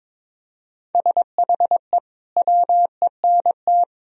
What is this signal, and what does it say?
Morse code, heard as a keyed tone, about 22 wpm, 700 Hz. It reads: SHE WENT
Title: Sentences from Top 1000 Words - Review Part 1 - 22wpm